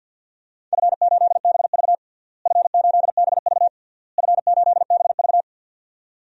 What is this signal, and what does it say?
Morse code, heard as a keyed tone, about 51 wpm, 700 Hz. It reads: F8BV F8BV F8BV
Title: DXCC-100 Call Signs - 4 Characters - Repeated 3x - 50wpm